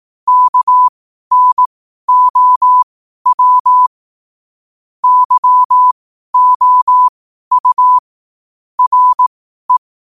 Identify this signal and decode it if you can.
Morse code, heard as a keyed tone, about 18 wpm, spaced as for 12 wpm, 1000 Hz. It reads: KNOW YOU RE
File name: qotdfiles / 12wpm